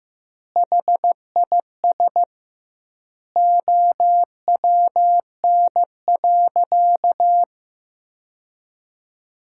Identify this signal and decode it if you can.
Morse code, heard as a keyed tone, about 15 wpm, 700 Hz. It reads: HIS OWN.
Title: 2x Word-Spacing - Sentences from Top 300 Words - 15wpm